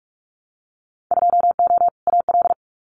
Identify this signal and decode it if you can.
Morse code, heard as a keyed tone, about 46 wpm, 700 Hz. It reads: 2O AL
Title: Winter Field Day Exchanges 45wpm